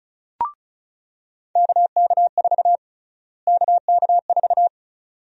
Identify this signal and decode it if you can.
Morse code, heard as a keyed tone, about 36 wpm, 700 Hz. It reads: KK4 KK4
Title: USA Call Sign Prefixes - Repeated 2x - 35wpm